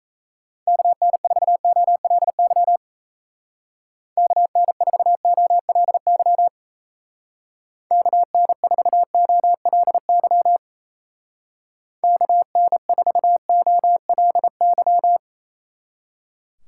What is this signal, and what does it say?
Morse code, heard as a keyed tone, about 35 wpm, 700 Hz. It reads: KN4OLY KN4OLY KN4OLY KN4OLY
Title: Speed Racing - USA 2x3 Call Signs 28wpm